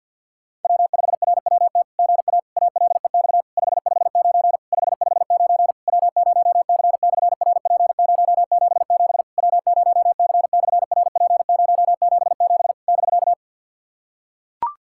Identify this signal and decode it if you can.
Morse code, heard as a keyed tone, about 48 wpm, 690 Hz. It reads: W5RWT GA ALEX 559 559 W0C/RP077 W0C/RP077 <BK>
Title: SOTA QSO Practice 50wpm